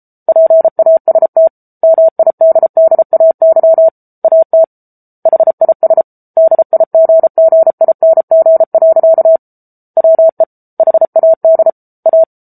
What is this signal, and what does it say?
Morse code, heard as a keyed tone, about 34 wpm, 650 Hz. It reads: PAST MIDDAY AT HIS DIGGING. WE HAD A
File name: Book Two / 07-Chapter 7 - 33 WPM